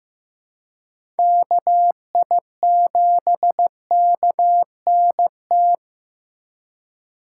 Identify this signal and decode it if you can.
Morse code, heard as a keyed tone, about 15 wpm, 700 Hz. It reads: KI7KNT
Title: USA Call Signs - 6 Characters - 2x Character Spacing - 15wpm